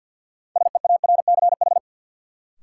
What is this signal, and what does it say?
Morse code, heard as a keyed tone, about 51 wpm, 700 Hz. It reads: SEARCH
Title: Sentences from Top 800 Words - New Word Drill - 50wpm